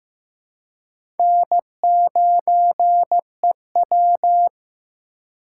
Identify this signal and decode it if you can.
Morse code, heard as a keyed tone, about 15 wpm, 700 Hz. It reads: N9EW
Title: USA Call Signs - 4 Characters - 3x Character Spacing - 15wpm